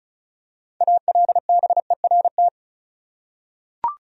Morse code, heard as a keyed tone, about 36 wpm, 700 Hz. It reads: ALBERT
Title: USA Common Names - 35wpm